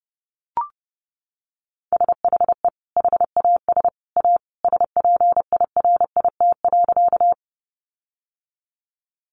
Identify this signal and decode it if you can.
Morse code, heard as a keyed tone, about 30 wpm, 700 Hz. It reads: SHE HAS A SPIRIT.
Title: Sentences from Top 1400 Words 30wpm